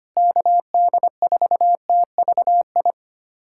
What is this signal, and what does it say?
Morse code, heard as a keyed tone, about 25 wpm, 700 Hz. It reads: KD4TVI
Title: USA Callsign DE Callsign 25wpm